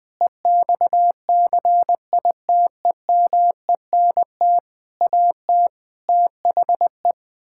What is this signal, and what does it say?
Morse code, heard as a keyed tone, about 21 wpm, 700 Hz. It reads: EXCITEMENT AT THE
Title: Sentences with 7-Word Phrases - 20wpm